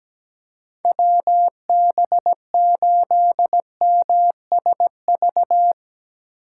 Morse code, heard as a keyed tone, about 17 wpm, 700 Hz. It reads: WB8MSV
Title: Speed Racing - USA 2x3 Call Signs 17wpm